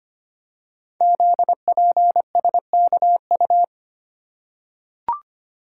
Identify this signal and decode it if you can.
Morse code, heard as a keyed tone, about 25 wpm, 700 Hz. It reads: ZPSKU
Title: Code Groups - 5 Letter:Numbers - 25wpm